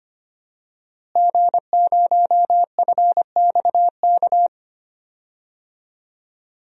Morse code, heard as a keyed tone, about 25 wpm, 700 Hz. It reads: G0FXK
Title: DXCC-100 Call Signs - 5 Characters - Repeated 2x - 25wpm